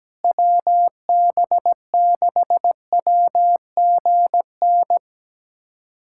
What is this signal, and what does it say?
Morse code, heard as a keyed tone, about 17 wpm, 700 Hz. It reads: WB6WGN
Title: USA Callsign DE Callsign - 6 Characters - 17wpm